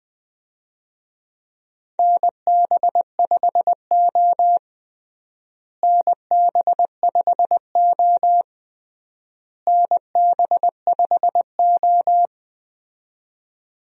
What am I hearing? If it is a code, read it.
Morse code, heard as a keyed tone, about 20 wpm, 700 Hz. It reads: NB5O NB5O NB5O
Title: USA Call Signs - 4 Characters - Repeated 3x - 20wpm